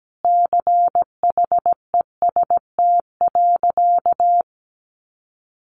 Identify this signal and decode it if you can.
Morse code, heard as a keyed tone, about 17 wpm, 700 Hz. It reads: CHEST.
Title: A Sound Of Thunder 17wpm